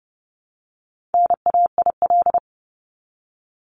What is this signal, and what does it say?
Morse code, heard as a keyed tone, about 30 wpm, 700 Hz. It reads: NAIL
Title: Top 1501-1750 Words 30wpm